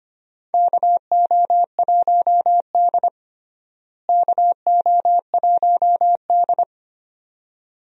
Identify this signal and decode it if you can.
Morse code, heard as a keyed tone, about 25 wpm, 700 Hz. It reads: KO1D KO1D
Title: USA Call Signs - 4 Characters - Repeated 2x - 25wpm